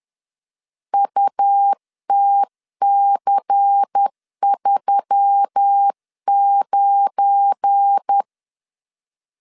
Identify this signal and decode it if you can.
Morse code, heard as a keyed tone, about 10 wpm, 800 Hz. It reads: UTC39
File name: LEZIONI / 9 28 10wpm 2842